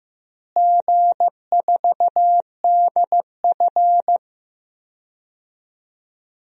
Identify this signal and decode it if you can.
Morse code, heard as a keyed tone, about 15 wpm, 700 Hz. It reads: G4DF
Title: DXCC-100 Call Signs - 4 Characters - Repeated 3x - 15wpm